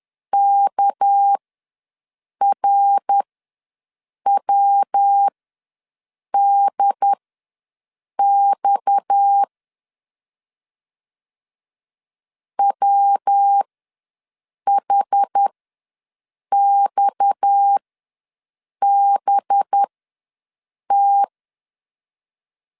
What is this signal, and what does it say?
Morse code, heard as a keyed tone, about 10 wpm, spaced as for 6 wpm, 800 Hz. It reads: KRWDX WHXBT
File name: LEZIONI / 6 PKX0 / 6 PKX ws 10wpm 3706